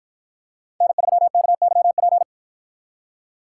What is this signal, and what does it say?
Morse code, heard as a keyed tone, about 50 wpm, 700 Hz. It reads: N3KYP